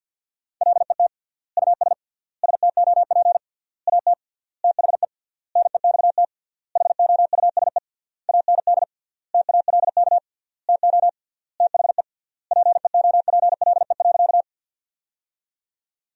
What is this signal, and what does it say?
Morse code, heard as a keyed tone, about 50 wpm, 700 Hz. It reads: LET US STOP AT THE NEXT HOUSE AND TALK TO THE PEOPLE.
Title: 3x Word-Spacing - Sentences from Top 400 Words - 50wpm